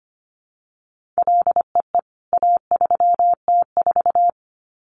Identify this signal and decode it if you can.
Morse code, heard as a keyed tone, about 25 wpm, 700 Hz. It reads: LEE A3T4
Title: CWT Exchange 25wpm